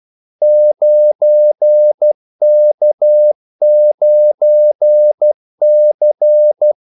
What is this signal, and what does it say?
Morse code, heard as a keyed tone, about 12 wpm, 600 Hz. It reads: 9K9C
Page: 2 ekstra ordmellemrum mellem kaldesignalerne